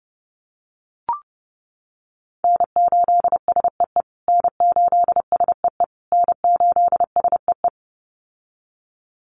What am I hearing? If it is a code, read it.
Morse code, heard as a keyed tone, about 30 wpm, 700 Hz. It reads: N8SEE N8SEE N8SEE
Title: USA Call Signs Repeated 3x 30wpm